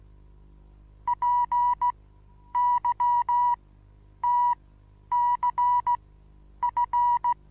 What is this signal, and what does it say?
Morse code, heard as a keyed tone, about 16 wpm, spaced as for 9 wpm, 970 Hz. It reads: PYTCF
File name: Onlinelehrgang / morsen / DJ4UF Morsen25